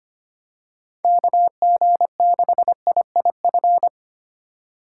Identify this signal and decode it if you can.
Morse code, heard as a keyed tone, about 25 wpm, 700 Hz. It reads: KG6IIF
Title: USA Call Signs - 6 Characters - 3x Character Spacing - 25wpm